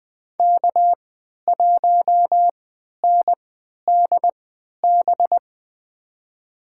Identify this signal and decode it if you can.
Morse code, heard as a keyed tone, about 21 wpm, spaced as for 11 wpm, 700 Hz. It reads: K1NDB
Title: USA Call Signs - 5 Characters - 3x Character Spacing - 20wpm